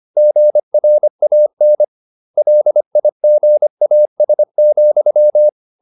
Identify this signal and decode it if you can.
Morse code, heard as a keyed tone, about 25 wpm, 600 Hz. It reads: GRAN LIGAS,
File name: the old man and the sea part 1 25wpm